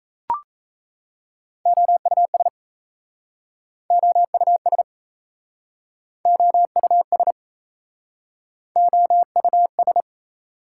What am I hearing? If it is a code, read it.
Morse code, heard as a keyed tone, about 38 wpm, 700 Hz. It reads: OUS OUS OUS OUS